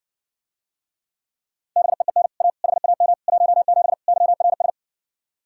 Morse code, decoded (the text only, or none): BEEN A HAM 17 YRS